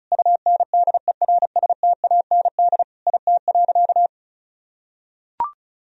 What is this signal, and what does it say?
Morse code, heard as a keyed tone, about 36 wpm, 700 Hz. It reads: UNDERSTAND IT.
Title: Sentences from Top 1000 Words - Review Part 2 - 35wpm